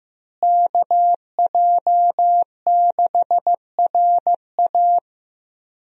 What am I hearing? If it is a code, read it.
Morse code, heard as a keyed tone, about 15 wpm, 700 Hz. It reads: KJ6RA
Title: USA Call Signs - 5 Characters - Repeated 3x - 15wpm